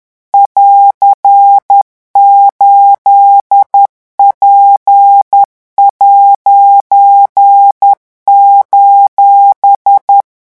Morse code, heard as a keyed tone, about 10 wpm, 800 Hz. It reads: <AR>8P':
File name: morse mp3 / revision19-vitesse 10 mots